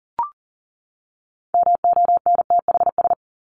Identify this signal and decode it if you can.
Morse code, heard as a keyed tone, about 40 wpm, 700 Hz. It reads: MONTHS
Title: ES:S Suffix - Top 100 Words with Base Words 40wpm